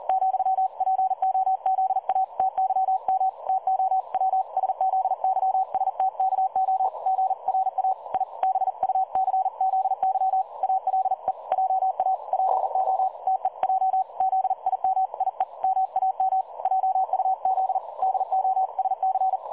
Morse code, heard as a keyed tone, about 38 wpm, 760 Hz. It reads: ,PO ZATYM TO WSZYSTKO GRA I BUCZY ALE JA PO TEJ ZIMIE MAM JUZ DOSC
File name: dlazaawansowanych / nagraniazpasma / websdr recording 3565.0kHz